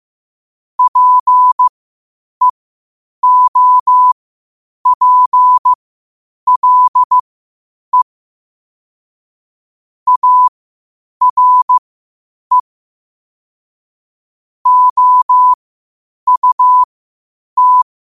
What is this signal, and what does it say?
Morse code, heard as a keyed tone, about 15 wpm, spaced as for 8 wpm, 1000 Hz. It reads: PEOPLE ARE OUT
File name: qotdfiles / 8wpm